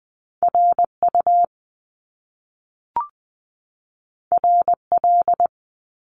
Morse code, heard as a keyed tone, about 20 wpm, 700 Hz. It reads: R U R L
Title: Top Two Letter Combinations in Words 20wpm